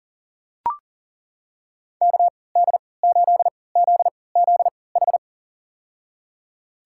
Morse code, heard as a keyed tone, about 39 wpm, spaced as for 22 wpm, 700 Hz. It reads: KD8ZZH
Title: USA Call Signs - 6 Characters - 3x Character Spacing - 40wpm